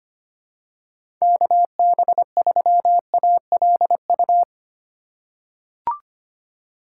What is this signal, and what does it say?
Morse code, heard as a keyed tone, about 25 wpm, 700 Hz. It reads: KB3ALU
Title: USA Call Signs - 6 Characters - 2x Character Spacing - 25wpm